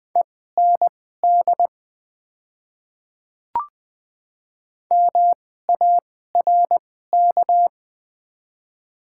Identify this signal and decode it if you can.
Morse code, heard as a keyed tone, about 20 wpm, spaced as for 14 wpm, 690 Hz. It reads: END MARK
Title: 2x Character Spacing - Top 300 Words - 20wpm